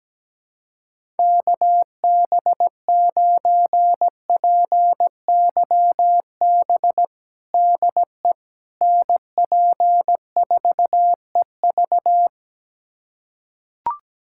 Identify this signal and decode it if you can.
Morse code, heard as a keyed tone, about 17 wpm, 700 Hz. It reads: KB9PYB DE NP4EV
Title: USA Callsign DE Callsign 17wpm